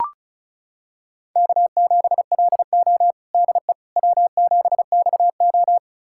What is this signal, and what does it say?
Morse code, heard as a keyed tone, about 36 wpm, 700 Hz. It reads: K7LO DE W7XO
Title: USA Callsign DE Callsign - 4 Characters - 35wpm